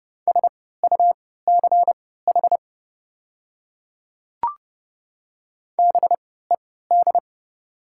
Morse code, heard as a keyed tone, about 31 wpm, spaced as for 17 wpm, 690 Hz. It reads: SUCH BED
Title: 3x Character Spacing - Top 500 Words - 30wpm